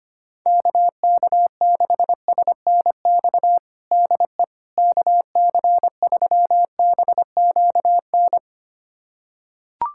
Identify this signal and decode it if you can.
Morse code, heard as a keyed tone, about 25 wpm, 700 Hz. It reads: KK6SNX DE KC3BQN
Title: USA Callsign DE Callsign - 6 Characters - 25wpm